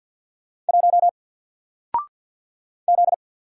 Morse code, heard as a keyed tone, about 50 wpm, 700 Hz. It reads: 1 Z
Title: Single Letter-Number 50wpm